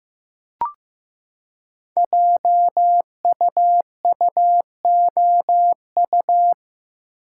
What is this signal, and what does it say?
Morse code, heard as a keyed tone, about 15 wpm, 700 Hz. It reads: JUUOU
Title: Code Groups - 5 Letters - 15wpm